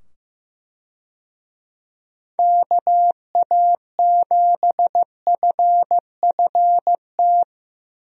Morse code, heard as a keyed tone, about 15 wpm, 700 Hz. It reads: KA7FFT
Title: USA Call Signs - 6 Characters - 15wpm